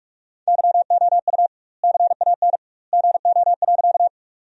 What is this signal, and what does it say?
Morse code, heard as a keyed tone, about 41 wpm, 700 Hz. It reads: YOU CAN GO.
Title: 2x Word-Spacing - Sentences from Top 100 Words - 45wpm